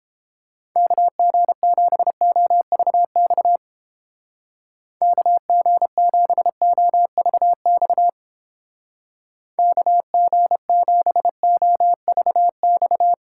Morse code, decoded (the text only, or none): KG7OVX KG7OVX KG7OVX